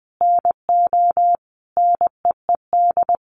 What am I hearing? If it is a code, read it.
Morse code, heard as a keyed tone, about 20 wpm, 700 Hz. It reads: NO NEED
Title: Sentences from Top 1300 Words 20wpm